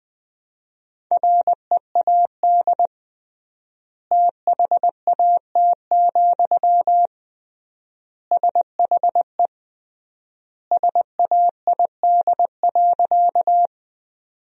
Morse code, heard as a keyed tone, about 20 wpm, 700 Hz. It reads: READ THAT, SHE SAID.
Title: 3x Word-Spacing - Sentences from Top 200 Words - 20wpm